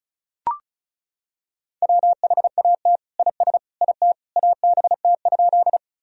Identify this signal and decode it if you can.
Morse code, heard as a keyed tone, about 36 wpm, 700 Hz. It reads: WHAT IS IT ABT?
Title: Sentences from Top 100 Words and Abbreviations 35wpm